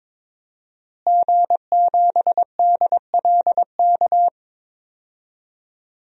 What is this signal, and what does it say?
Morse code, heard as a keyed tone, about 22 wpm, 700 Hz. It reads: G7DLK
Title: DXCC-100 Call Signs - 5 Characters - Repeated 2x - 22wpm